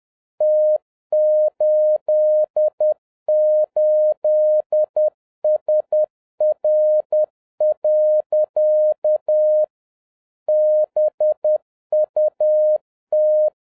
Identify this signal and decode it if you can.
Morse code, heard as a keyed tone, about 10 wpm, 610 Hz. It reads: T88SR. BUT